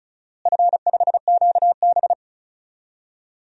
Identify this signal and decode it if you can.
Morse code, heard as a keyed tone, about 34 wpm, 700 Hz. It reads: F5QB
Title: DXCC-100 Call Signs - 4 Characters - Repeated 3x - 35wpm